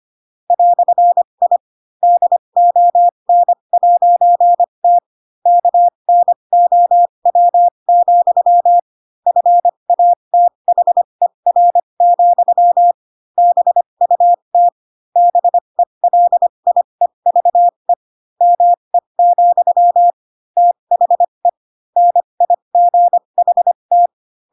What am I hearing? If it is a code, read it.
Morse code, heard as a keyed tone, about 25 wpm, spaced as for 22 wpm, 700 Hz. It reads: "I DON'T KNOW, FATHER, BUT BELIEVE ME, THE NIGHT
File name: Pinocchio / Pinocchio ch7 18 25 ik1hge